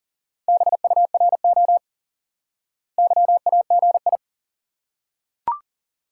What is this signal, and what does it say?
Morse code, heard as a keyed tone, about 40 wpm, 700 Hz. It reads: BURO YAGI
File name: IQE Course 3 - Lesson 89 - YAGI - Two Elements - 40wpm